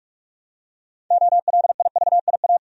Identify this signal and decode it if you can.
Morse code, heard as a keyed tone, about 46 wpm, 700 Hz. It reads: OLIVIA
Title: USA Common Names - 45wpm